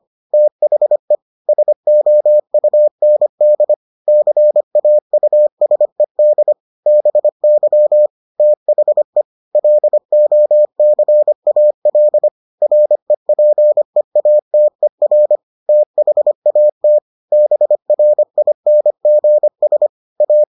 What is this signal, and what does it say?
Morse code, heard as a keyed tone, about 25 wpm, 600 Hz. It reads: THE SOUND CAUSED BY THE LOCAL REPEATER THAT BRINGS A